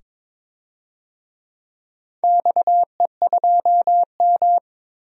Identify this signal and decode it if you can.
Morse code, heard as a keyed tone, about 22 wpm, 700 Hz. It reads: XE2M
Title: DXCC-100 Call Signs - 4 Characters - 22wpm